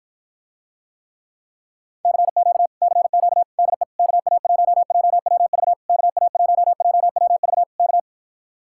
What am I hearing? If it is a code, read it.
Morse code, heard as a keyed tone, about 54 wpm, 700 Hz. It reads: CQ CQ DE KA1JWV KA1JWV K